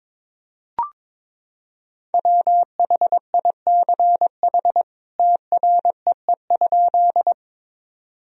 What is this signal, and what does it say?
Morse code, heard as a keyed tone, about 22 wpm, 700 Hz. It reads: WHICH TREE?
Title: Sentences from Top 1000 Words - Review Part 1 - 22wpm